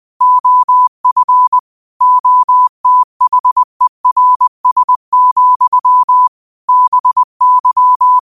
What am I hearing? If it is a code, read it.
Morse code, heard as a keyed tone, about 20 wpm, 1000 Hz. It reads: OF OTHERS, BY
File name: qotdfiles / 20wpm